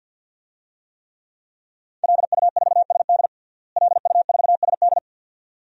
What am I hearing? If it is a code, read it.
Morse code, heard as a keyed tone, about 47 wpm, 700 Hz. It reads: LU4SD LU4SD
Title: DXCC-100 Call Signs - 5 Characters - Repeated 2x - 50wpm